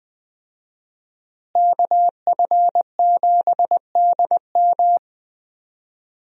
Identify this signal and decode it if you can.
Morse code, heard as a keyed tone, about 21 wpm, 700 Hz. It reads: KF7DM
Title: USA Call Signs - 5 Characters - Repeated 3x - 20wpm